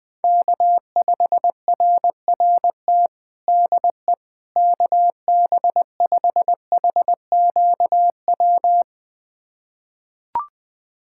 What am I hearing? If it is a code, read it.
Morse code, heard as a keyed tone, about 20 wpm, 700 Hz. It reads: K5RRT DE KB5HQW
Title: USA Callsign DE Callsign 20wpm